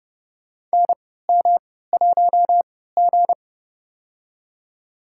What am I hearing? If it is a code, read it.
Morse code, heard as a keyed tone, about 31 wpm, spaced as for 17 wpm, 700 Hz. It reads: NM1G
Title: USA Call Signs - 4 Characters - 3x Character Spacing - 30wpm